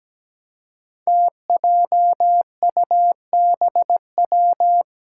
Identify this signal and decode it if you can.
Morse code, heard as a keyed tone, about 17 wpm, 700 Hz. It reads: TJUBW
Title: Five Challenging Letter-Number Combinations 17wpm